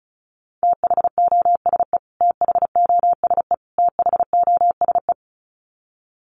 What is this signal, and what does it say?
Morse code, heard as a keyed tone, about 34 wpm, 700 Hz. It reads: THOSE THOSE THOSE
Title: Top 300 Words Repeated 35wpm